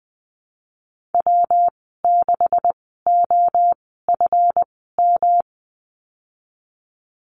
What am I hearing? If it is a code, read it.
Morse code, heard as a keyed tone, about 19 wpm, spaced as for 14 wpm, 700 Hz. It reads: W6OFM
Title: 147 Lesson 24g - Letter y - Call Signs 20wpm